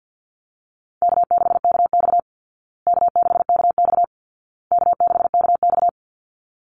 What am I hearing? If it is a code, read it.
Morse code, heard as a keyed tone, about 50 wpm, 700 Hz. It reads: K6KX K6KX K6KX
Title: Call Signs Repeated 2x - 50wpm